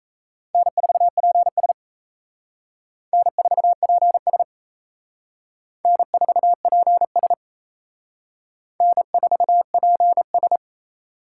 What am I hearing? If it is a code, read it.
Morse code, heard as a keyed tone, about 35 wpm, 700 Hz. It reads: N4PS N4PS N4PS N4PS
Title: Speed Racing - USA 1x2 Call Signs 28wpm